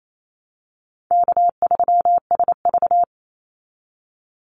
Morse code, heard as a keyed tone, about 28 wpm, 700 Hz. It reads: K3SV
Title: Easy Call Signs Repeated 2x - 28wpm